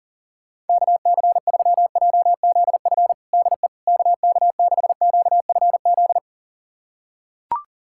Morse code, heard as a keyed tone, about 37 wpm, 700 Hz. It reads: KC3JZF DE KK6QFZ
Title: USA Callsign DE Callsign 40wpm